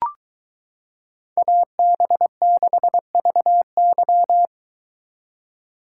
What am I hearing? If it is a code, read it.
Morse code, heard as a keyed tone, about 22 wpm, 700 Hz. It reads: AB6VY